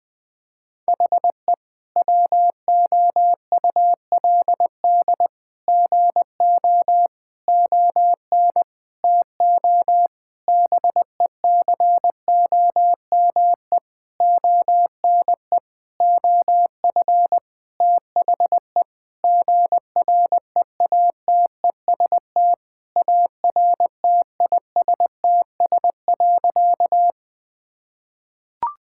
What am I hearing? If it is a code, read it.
Morse code, heard as a keyed tone, about 20 wpm, 700 Hz. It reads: HE WOULD GO ON TO BECOME ONE OF THE GREATEST ARTISTS.
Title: Sentences with 8-Word Phrases - Spoken Beginning - 20wpm